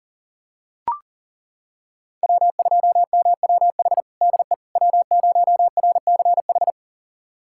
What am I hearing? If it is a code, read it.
Morse code, heard as a keyed tone, about 37 wpm, 700 Hz. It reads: W2MWH DE W0RCH
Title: USA Callsign DE Callsign - 5 Characters - 40wpm